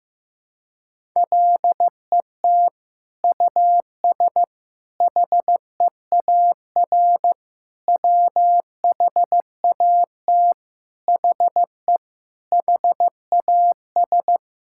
Morse code, heard as a keyed tone, about 16 wpm, 700 Hz. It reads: LET US HEAR WHAT HE HAS
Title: Sentences from Top 1600 Words 15wpm